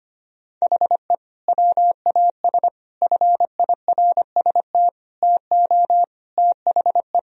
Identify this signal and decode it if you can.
Morse code, heard as a keyed tone, about 25 wpm, 700 Hz. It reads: HE WAS FIRST TO THE
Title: Sentences from Top 1000 Words - Review Part 2 - 25wpm